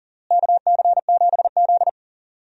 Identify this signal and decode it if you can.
Morse code, heard as a keyed tone, about 40 wpm, 700 Hz. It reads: KC7Z
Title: USA Call Signs - 4 Characters - Repeated 2x - 40wpm